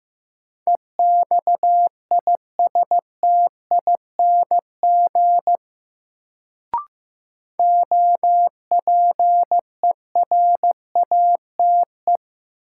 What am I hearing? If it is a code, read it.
Morse code, heard as a keyed tone, about 15 wpm, 700 Hz. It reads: EXISTING OPERATE
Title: Sentences from Top 1500 Words - New Word Drill - 15wpm